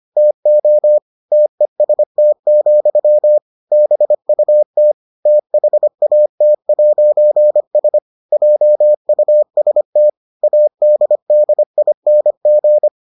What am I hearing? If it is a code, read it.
Morse code, heard as a keyed tone, about 25 wpm, 600 Hz. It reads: TO TEST, BUT THAT'S JUST ADDING